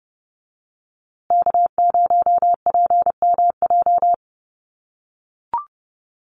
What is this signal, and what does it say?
Morse code, heard as a keyed tone, about 30 wpm, 700 Hz. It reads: K0PMJ
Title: USA Call Signs Repeated 3x 30wpm